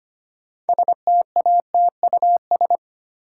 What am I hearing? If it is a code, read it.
Morse code, heard as a keyed tone, about 25 wpm, 700 Hz. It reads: STATUS
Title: Six Letter Words - 25wpm